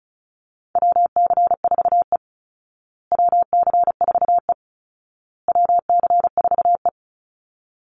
Morse code, heard as a keyed tone, about 34 wpm, 700 Hz. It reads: WC4E WC4E WC4E
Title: Call Signs Repeated 2x - 35wpm